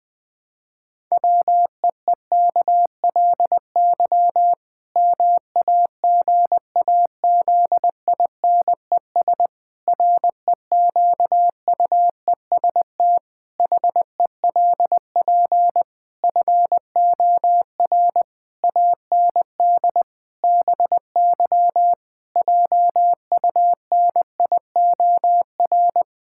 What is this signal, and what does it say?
Morse code, heard as a keyed tone, about 21 wpm, 700 Hz. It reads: WEEKLY MAGAZINES REQUEST HELP FOR AND BY JUNIOR
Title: Pangram Challenge v2 - 20wpm